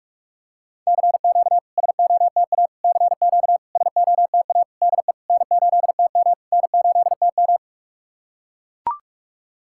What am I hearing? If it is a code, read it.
Morse code, heard as a keyed tone, about 44 wpm, 700 Hz. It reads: CQ SOTA CQ SOTA DE N8TM N8TM
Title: SOTA QSO Practice 45wpm